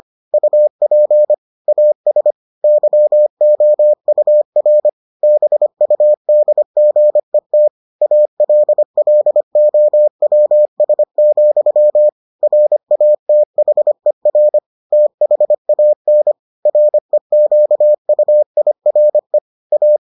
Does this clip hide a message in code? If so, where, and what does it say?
Yes: Morse code from 0.3 to 20.0 s, heard as a keyed tone, about 25 wpm, 600 Hz. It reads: UP AS YOUR BUDGET ALLOWS, RATHER THAN REQUIRE A